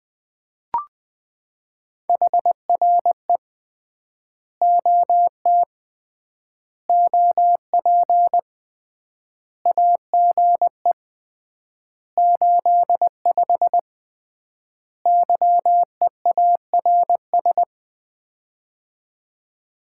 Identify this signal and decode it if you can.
Morse code, heard as a keyed tone, about 20 wpm, 700 Hz. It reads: HRE OT OP AGE 85 YEARS
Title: Common QSO Phrases - 3x Word-Spacing - 20wpm